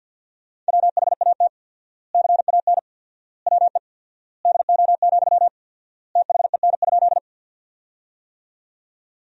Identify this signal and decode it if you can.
Morse code, heard as a keyed tone, about 52 wpm, 700 Hz. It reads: WHAT CAN WE DO, THEN?
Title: 4x Word-Spacing - Sentences from Top 100 Words - 50wpm